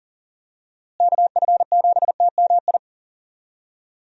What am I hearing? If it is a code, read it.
Morse code, heard as a keyed tone, about 40 wpm, 700 Hz. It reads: KF7TMI
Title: USA Call Signs - 6 Characters - Repeated 3x - 40wpm